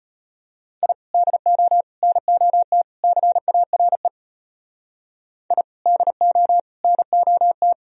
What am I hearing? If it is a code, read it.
Morse code, heard as a keyed tone, about 31 wpm, 700 Hz. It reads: I DO NOT CARE I DO NOT
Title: Speed-Racing - Sets of 4 Words 25wpm